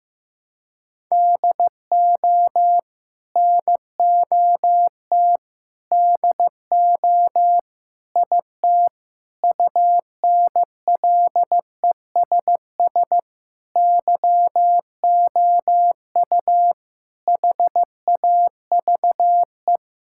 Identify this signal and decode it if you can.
Morse code, heard as a keyed tone, about 14 wpm, 700 Hz. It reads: DO NOT DO IT UNLESS YOU HAVE
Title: Sentences from Top 1000 Words - Review Part 6 - 15wpm